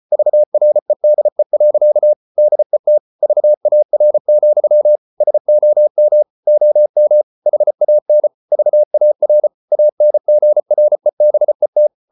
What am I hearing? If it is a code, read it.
Morse code, heard as a keyed tone, about 34 wpm, 600 Hz. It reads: VREDE. DET VAR, SOM OM HAN VAR ANGREBET
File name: Ferie QRQ 34wpm